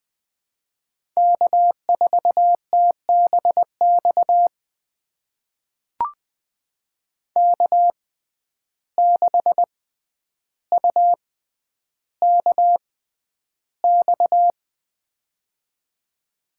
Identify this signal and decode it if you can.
Morse code, heard as a keyed tone, about 20 wpm, spaced as for 10 wpm, 700 Hz. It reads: K4TBX K6UKX
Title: USA Call Signs - 5 Characters - 6x Character Spacing - 20wpm